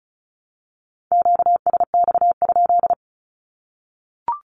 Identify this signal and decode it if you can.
Morse code, heard as a keyed tone, about 34 wpm, 700 Hz. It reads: QSX?
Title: Q-Codes Repeated 3x 35wpm